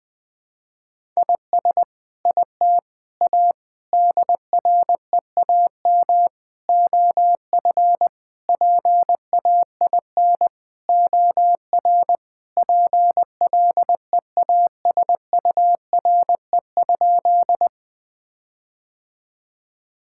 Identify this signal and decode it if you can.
Morse code, heard as a keyed tone, about 20 wpm, 700 Hz. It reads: IS IT A DREAM OF PAIN OR PLEASURE?
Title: Sentences from Top 1600 Words 20wpm